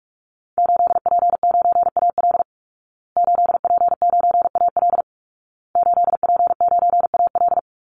Morse code, heard as a keyed tone, about 44 wpm, 700 Hz. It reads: 8P9AL 8P9AL 8P9AL
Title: Call Signs Repeated 2x - 45wpm